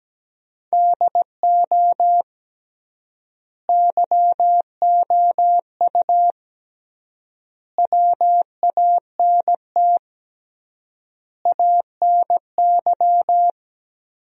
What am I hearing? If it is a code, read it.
Morse code, heard as a keyed tone, about 17 wpm, 700 Hz. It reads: DO YOU WANT ANY
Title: 3x Word-Spacing - Sentences from Top 400 Words - 17wpm